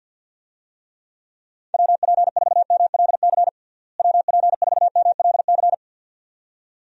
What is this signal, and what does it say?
Morse code, heard as a keyed tone, about 49 wpm, 700 Hz. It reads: WP4MLC WP4MLC
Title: USA Call Signs - 6 Characters - Repeated 2x - 50wpm